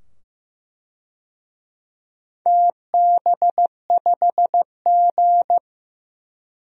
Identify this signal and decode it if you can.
Morse code, heard as a keyed tone, about 15 wpm, 700 Hz. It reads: TB5G
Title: DXCC-100 Call Signs - 4 Characters - 15wpm